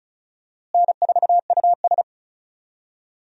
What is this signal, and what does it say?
Morse code, heard as a keyed tone, about 36 wpm, 700 Hz. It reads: N4US
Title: USA Call Signs - 4 Characters - Repeated 3x - 35wpm